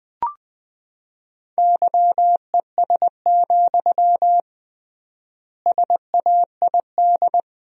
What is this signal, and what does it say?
Morse code, heard as a keyed tone, about 20 wpm, 700 Hz. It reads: YES, SAID
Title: 3x Word-Spacing - Sentences from Top 500 Words - 20wpm